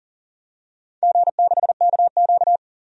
Morse code, heard as a keyed tone, about 41 wpm, 700 Hz. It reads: G6KQ
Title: DXCC-100 Call Signs - 4 Characters - 40wpm